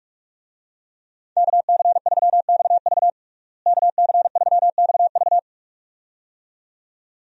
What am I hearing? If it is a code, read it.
Morse code, heard as a keyed tone, about 46 wpm, 700 Hz. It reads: KC3XV KC3XV
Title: USA Call Signs - 5 Characters - Repeated 2x - 45wpm